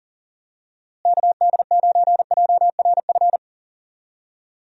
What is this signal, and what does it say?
Morse code, heard as a keyed tone, about 40 wpm, 700 Hz. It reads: KD9JRF
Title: USA Call Signs - 6 Characters - Repeated 3x - 40wpm